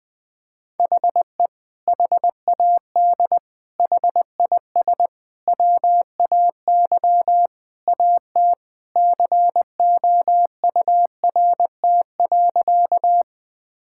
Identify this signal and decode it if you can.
Morse code, heard as a keyed tone, about 21 wpm, 700 Hz. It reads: HE HAD HIS WAY AT COURT.
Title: Sentences from Top 1600 Words 20wpm